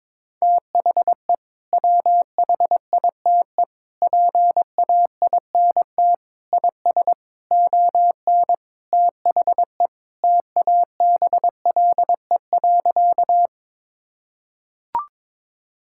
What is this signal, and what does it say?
Morse code, heard as a keyed tone, about 22 wpm, 700 Hz. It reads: THE WHITE PAINT IS ON THE TABLE.
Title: Sentences from Top 1000 Words - Review Part 6 - 22wpm